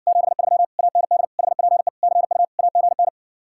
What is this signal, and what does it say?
Morse code, heard as a keyed tone, about 60 wpm, 700 Hz. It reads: 73 AND HPE CU AGN